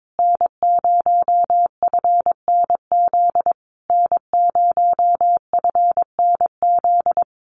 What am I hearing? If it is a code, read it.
Morse code, heard as a keyed tone, about 22 wpm, 700 Hz. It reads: N0FNZ N0FNZ
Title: Common QSO Phrases Encore 22wpm